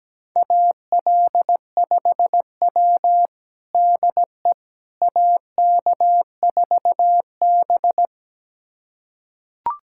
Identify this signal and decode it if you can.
Morse code, heard as a keyed tone, about 17 wpm, 700 Hz. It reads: AL5W DE AK4B
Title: USA Callsign DE Callsign - 4 Characters - 17wpm